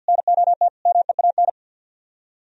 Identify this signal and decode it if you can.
Morse code, heard as a keyed tone, about 64 wpm, 700 Hz. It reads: NOT MEAN